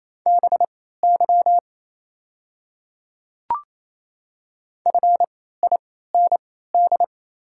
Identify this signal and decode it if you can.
Morse code, heard as a keyed tone, about 27 wpm, spaced as for 16 wpm, 690 Hz. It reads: BY FIND
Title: 3x Character Spacing - Top 300 Words - 28wpm